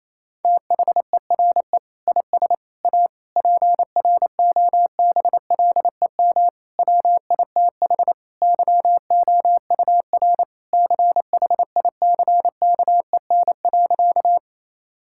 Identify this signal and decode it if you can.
Morse code, heard as a keyed tone, about 28 wpm, 700 Hz. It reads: THERE IS A PROBLEM WITH YOUR CHICKEN.
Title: Sentences from Top 1000 Words - Review Part 5 - 28wpm